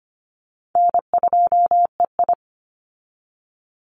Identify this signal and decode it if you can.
Morse code, heard as a keyed tone, about 25 wpm, 700 Hz. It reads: N2EI
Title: Call Signs Repeated 2x - 25wpm